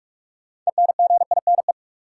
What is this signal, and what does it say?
Morse code, heard as a keyed tone, about 46 wpm, 700 Hz. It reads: ENGINE
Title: Six Letter Words - 45wpm